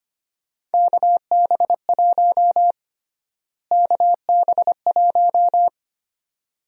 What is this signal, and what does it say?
Morse code, heard as a keyed tone, about 25 wpm, 700 Hz. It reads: KB1 KB1
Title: USA Call Sign Prefixes - Repeated 2x - 25wpm